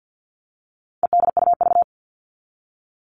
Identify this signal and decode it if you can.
Morse code, heard as a keyed tone, about 50 wpm, 700 Hz. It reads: EDUV
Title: All Cut Numbers - 4-Digit 50wpm